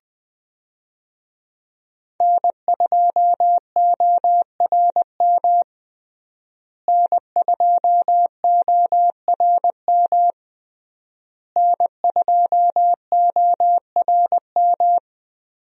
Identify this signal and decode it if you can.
Morse code, heard as a keyed tone, about 21 wpm, 700 Hz. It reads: N2ORM N2ORM N2ORM
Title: USA Call Signs - 5 Characters - Repeated 3x - 20wpm